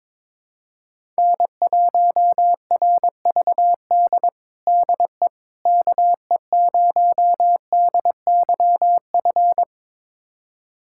Morse code, heard as a keyed tone, about 22 wpm, 700 Hz. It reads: N1RVD DE KE0DYF
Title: USA Callsign DE Callsign 22wpm